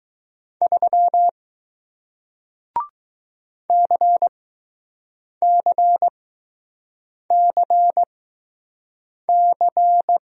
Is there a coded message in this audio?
Yes: Morse code from 0.6 to 10.2 s, heard as a keyed tone, about 20 wpm, spaced as for 6 wpm, 700 Hz. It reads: 3 CCCC